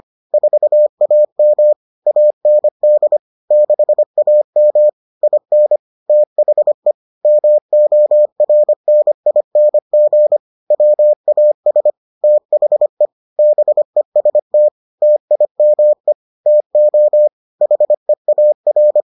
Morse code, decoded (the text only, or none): 4AM AND 6AM IN THE MORNING WAS THE BEST TIME TO HEAR